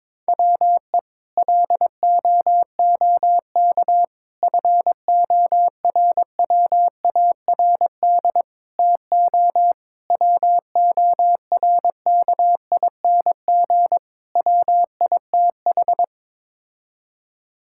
Morse code, heard as a keyed tone, about 22 wpm, 700 Hz. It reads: WE LOOK FORWARD TO WORKING WITH
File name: practice / 22wpm / Sets of 6 Words - 22wpm